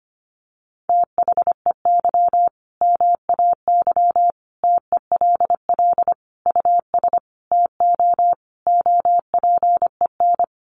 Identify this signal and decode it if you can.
Morse code, heard as a keyed tone, about 25 wpm, 700 Hz. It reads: THEY MAY TELL US TO OPEN
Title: Sentences from Top 300 Words 25wpm